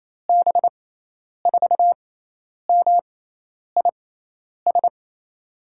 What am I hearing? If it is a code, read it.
Morse code, heard as a keyed tone, about 28 wpm, spaced as for 10 wpm, 700 Hz. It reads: B4MIS
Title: USA Call Signs - 6 Characters - 6x Character Spacing - 28wpm